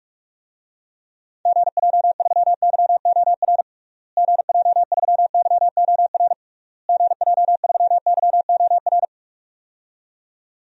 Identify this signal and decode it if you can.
Morse code, heard as a keyed tone, about 44 wpm, 700 Hz. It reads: GJ3YOR GJ3YOR GJ3YOR
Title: DXCC-100 Call Signs - 6 Characters - Repeated 3x - 45wpm